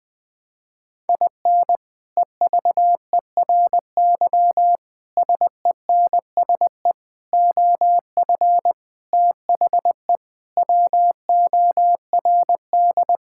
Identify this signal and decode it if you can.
Morse code, heard as a keyed tone, about 20 wpm, 700 Hz. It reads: IN EVERY SENSE OF THE WORD
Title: Sets of 6 Words - 20wpm